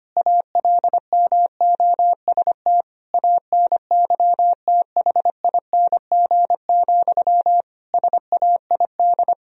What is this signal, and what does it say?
Morse code, heard as a keyed tone, about 25 wpm, 700 Hz. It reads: ALMOST ANYTHING, SAID
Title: Sentences from Top 1000 Words - Review Part 4 - 25wpm